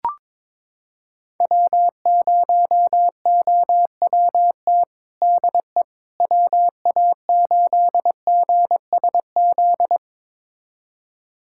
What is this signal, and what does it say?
Morse code, heard as a keyed tone, about 22 wpm, 700 Hz. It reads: W0OWT DE WA8GSZ
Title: USA Callsign DE Callsign 22wpm